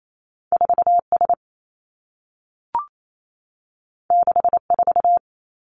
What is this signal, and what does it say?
Morse code, heard as a keyed tone, about 28 wpm, 700 Hz. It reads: 4 S 6 4
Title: Two Challenging Letter-Number Combinations 28wpm